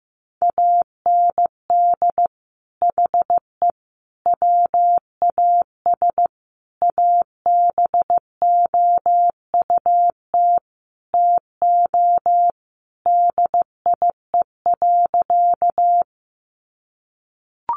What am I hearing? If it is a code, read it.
Morse code, heard as a keyed tone, about 15 wpm, 700 Hz. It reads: AND HE WAS ABOUT TO DIE.
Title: Sentences from Top 1300 Words 15wpm